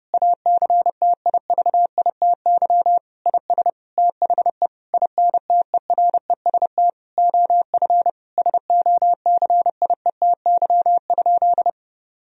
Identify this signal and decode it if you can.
Morse code, heard as a keyed tone, about 28 wpm, 700 Hz. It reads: ACTIVITY IS THE INTEREST OF SOCIETY?
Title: Sentences from Top 1000 Words - Review Part 4 - 30wpm